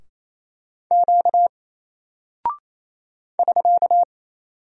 Morse code, heard as a keyed tone, about 28 wpm, 700 Hz. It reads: Q <SK>
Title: Single Character (Letter:Number:Punctuation:Prosign) Alt 28wpm